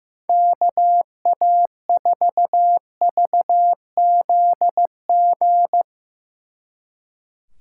Morse code, heard as a keyed tone, about 15 wpm, 700 Hz. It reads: KA4VZG
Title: Speed Racing - USA Call Signs 15wpm